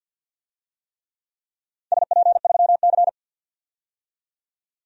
Morse code, heard as a keyed tone, about 51 wpm, 700 Hz. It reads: SP3C
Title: DXCC-100 Call Signs - 4 Characters - 50wpm